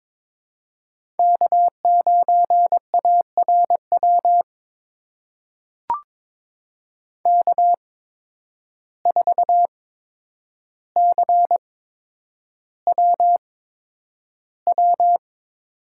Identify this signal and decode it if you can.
Morse code, heard as a keyed tone, about 22 wpm, spaced as for 9 wpm, 700 Hz. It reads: K9ARW K4CWW
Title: USA Call Signs - 5 Characters - 8x Character Spacing - 22wpm